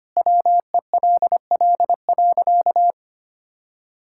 Morse code, heard as a keyed tone, about 25 wpm, 700 Hz. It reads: WELL.
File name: practice / 25wpm / Sentences with 2-Word Phrases - 25wpm